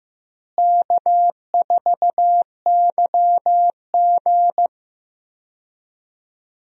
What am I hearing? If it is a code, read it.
Morse code, heard as a keyed tone, about 15 wpm, 700 Hz. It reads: K4YG
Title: USA Call Signs - 4 Characters - Repeated 2x - 15wpm